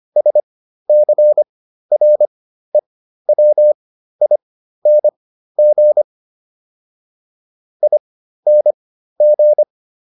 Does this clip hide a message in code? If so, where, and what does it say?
Yes: Morse code from 0.2 to 9.6 s, heard as a keyed tone, about 25 wpm, spaced as for 13 wpm, 600 Hz. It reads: SCREWING ING